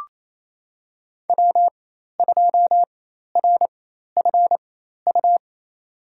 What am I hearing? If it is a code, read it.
Morse code, heard as a keyed tone, about 28 wpm, spaced as for 13 wpm, 700 Hz. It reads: W2RFU
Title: USA Call Signs - 5 Characters - 4x Character Spacing - 28wpm